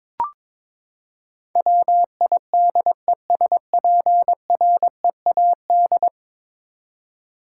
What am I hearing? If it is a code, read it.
Morse code, heard as a keyed tone, about 21 wpm, 700 Hz. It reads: WIDESPREAD
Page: morse-code-ninja-practice-22wpm